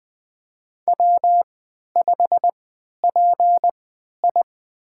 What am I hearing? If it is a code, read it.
Morse code, heard as a keyed tone, about 20 wpm, spaced as for 11 wpm, 700 Hz. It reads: W5PI
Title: USA Call Signs - 4 Characters - 3x Character Spacing - 20wpm